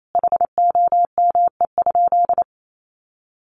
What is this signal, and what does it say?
Morse code, heard as a keyed tone, about 28 wpm, 700 Hz. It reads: HOME?
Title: Sentences from Top 800 Words 28wpm